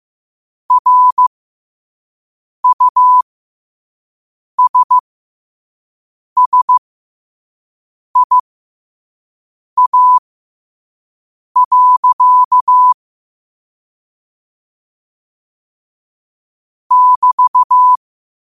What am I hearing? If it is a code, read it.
Morse code, heard as a keyed tone, about 15 wpm, spaced as for 5 wpm, 1000 Hz. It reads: RUSSIA. <BT>
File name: qotdfiles / 5wpm